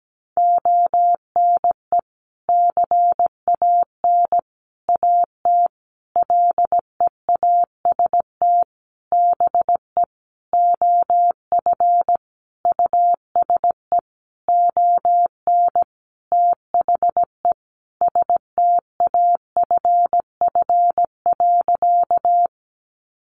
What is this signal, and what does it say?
Morse code, heard as a keyed tone, about 17 wpm, 700 Hz. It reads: ONE CAN AT LEAST BE OF USE ON THE STAFF.
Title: Sentences from Top 1200 Words 17wpm